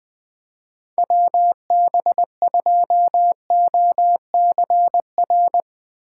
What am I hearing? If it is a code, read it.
Morse code, heard as a keyed tone, about 21 wpm, 700 Hz. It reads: WB2OCR
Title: Speed Racing - USA 2x3 Call Signs 17wpm